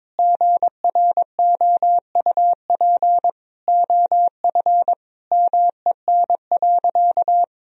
Morse code, heard as a keyed tone, about 22 wpm, 700 Hz. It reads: GROUP OF MEN.
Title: Sentences from Top 1000 Words - Review Part 1 - 22wpm